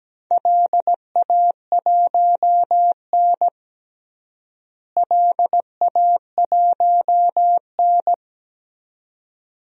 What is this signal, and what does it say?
Morse code, heard as a keyed tone, about 17 wpm, 700 Hz. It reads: LA1N LA1N
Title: DXCC-100 Call Signs - 4 Characters - Repeated 2x - 17wpm